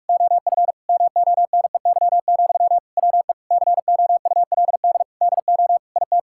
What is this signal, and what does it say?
Morse code, heard as a keyed tone, about 42 wpm, 700 Hz. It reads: OF MONEY, WE COULD DO IT
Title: Sentences from Top 1600 Words 45wpm